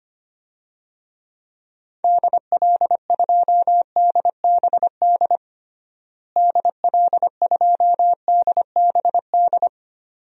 Morse code, heard as a keyed tone, about 25 wpm, 700 Hz. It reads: DL2DBD DL2DBD
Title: DXCC-100 Call Signs - 6 Characters - Repeated 2x - 25wpm